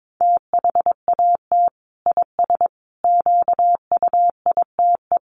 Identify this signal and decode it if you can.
Morse code, heard as a keyed tone, about 22 wpm, 700 Hz. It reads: THAT IS QUITE
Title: Sentences from Top 1100 Words 22wpm